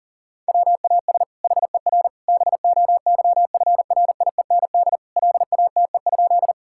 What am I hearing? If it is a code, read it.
Morse code, heard as a keyed tone, about 40 wpm, 690 Hz. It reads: WAS HER BOYFRIEND LATE?
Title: Sentences from Top 1000 Words - Review Part 5 - 40wpm